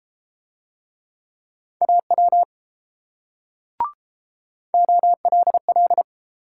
Morse code, heard as a keyed tone, about 32 wpm, 690 Hz. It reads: AW OLL